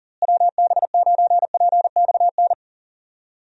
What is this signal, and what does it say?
Morse code, heard as a keyed tone, about 40 wpm, 700 Hz. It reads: WB9PXN
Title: USA Call Signs - 6 Characters - Repeated 3x - 40wpm